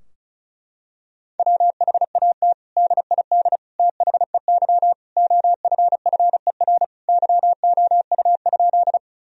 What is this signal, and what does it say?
Morse code, heard as a keyed tone, about 36 wpm, 700 Hz. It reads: WHAT DID THEY OFFER YOU?
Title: Sentences from Top 1000 Words - Review Part 4 - 35wpm